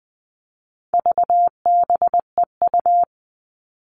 Morse code, heard as a keyed tone, about 20 wpm, 700 Hz. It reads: VBEU
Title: All Cut Numbers - 4-Digit 20wpm